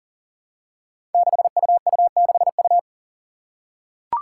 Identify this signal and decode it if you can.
Morse code, heard as a keyed tone, about 40 wpm, 700 Hz. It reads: 6UU6U
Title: Five Challenging Letter-Number Combinations 40wpm